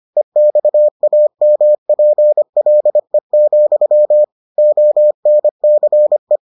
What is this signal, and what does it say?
Morse code, heard as a keyed tone, about 25 wpm, 600 Hz. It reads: EXAMPLE, ONCE